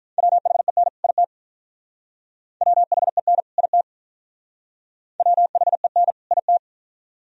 Morse code, heard as a keyed tone, about 44 wpm, 700 Hz. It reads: WHEN IT WHEN IT WHEN IT
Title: Speed-Racing - Sets of 2 Words 35wpm